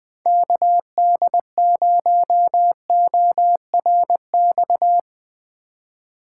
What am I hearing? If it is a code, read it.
Morse code, heard as a keyed tone, about 20 wpm, 700 Hz. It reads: KD0ORX
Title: Speed Racing - USA Call Signs 20wpm